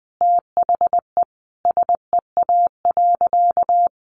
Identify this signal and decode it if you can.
Morse code, heard as a keyed tone, about 20 wpm, 700 Hz. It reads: THE SEA.
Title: Sentences from Top 1300 Words 20wpm